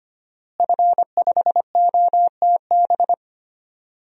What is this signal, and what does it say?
Morse code, heard as a keyed tone, about 25 wpm, 700 Hz. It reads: F5OTB
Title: DXCC-100 Call Signs - 5 Characters - Repeated 3x - 25wpm